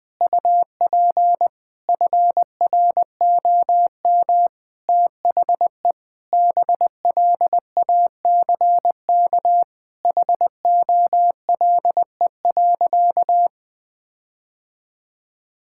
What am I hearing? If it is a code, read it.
Morse code, heard as a keyed tone, about 20 wpm, 700 Hz. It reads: UP FROM THE BLACK HOLE.
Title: Sentences from Top 1600 Words 20wpm